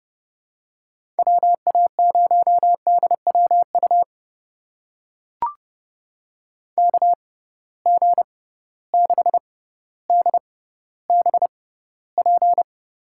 Morse code, heard as a keyed tone, about 32 wpm, spaced as for 16 wpm, 700 Hz. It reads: WA0DWU KG6DBP
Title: USA Call Signs - 6 Characters - 6x Character Spacing - 30wpm